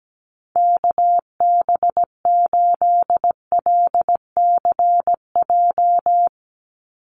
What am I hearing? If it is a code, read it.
Morse code, heard as a keyed tone, about 17 wpm, 700 Hz. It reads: KB8LCJ
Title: USA Call Signs 17wpm